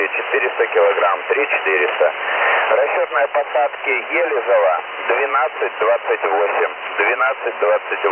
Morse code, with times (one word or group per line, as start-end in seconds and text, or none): none